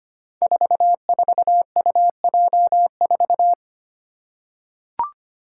Morse code, heard as a keyed tone, about 25 wpm, 700 Hz. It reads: 44UJ4
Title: Five Challenging Letter-Number Combinations 25wpm